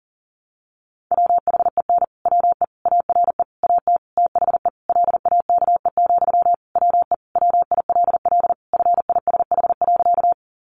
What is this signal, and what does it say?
Morse code, heard as a keyed tone, about 40 wpm, 700 Hz. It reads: WHEN WE ARE AT THE LAKE, WE WILL FISH.
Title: Sentences from Top 1100 Words 40wpm